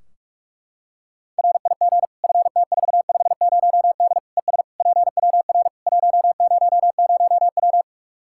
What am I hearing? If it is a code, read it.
Morse code, heard as a keyed tone, about 46 wpm, 700 Hz. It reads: RIG FT450D ES PWR 100W
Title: Practice QSO - Exchange 2 - Info - 45wpm